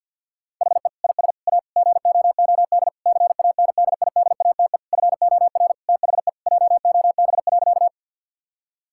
Morse code, heard as a keyed tone, about 50 wpm, 700 Hz. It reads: HE IS A GOOD CANDIDATE FOR THE JOB.
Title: Sentences from Top 1600 Words 50wpm